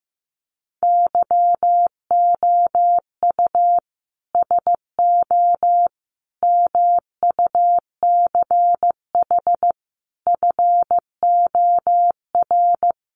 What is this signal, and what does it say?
Morse code, heard as a keyed tone, about 15 wpm, 700 Hz. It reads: YOU SO MUCH FOR
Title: Sets of 5 Words Encore2 15wpm